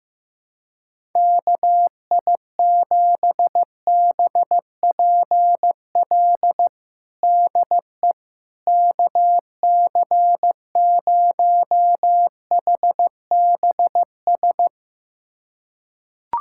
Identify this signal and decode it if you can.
Morse code, heard as a keyed tone, about 15 wpm, 700 Hz. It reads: KI7BPL DE KC0HBS
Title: USA Callsign DE Callsign - 6 Characters - 15wpm